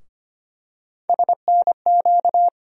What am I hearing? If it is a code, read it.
Morse code, heard as a keyed tone, about 25 wpm, 700 Hz. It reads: SNQ
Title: USA Call Sign Postfixes - 3 Letters 25wpm